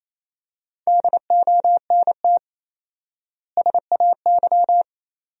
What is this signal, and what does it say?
Morse code, heard as a keyed tone, about 28 wpm, 700 Hz. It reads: DONT SAY
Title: 4x Word-Spacing - Sentences from Top 300 Words - 28wpm